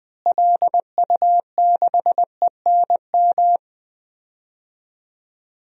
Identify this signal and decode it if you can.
Morse code, heard as a keyed tone, about 20 wpm, 700 Hz. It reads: LU6ENM
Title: DXCC-100 Call Signs - 6 Characters - Repeated 3x - 20wpm